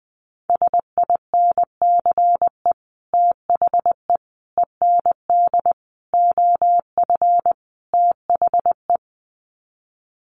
Morse code, heard as a keyed tone, about 20 wpm, 700 Hz. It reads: SINCE THE END OF THE
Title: Sets of 5 Words Encore 20wpm